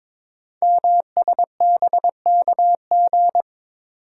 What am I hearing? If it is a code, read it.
Morse code, heard as a keyed tone, about 22 wpm, 700 Hz. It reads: MSBKG
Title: Code Groups - 5 Letters - 22wpm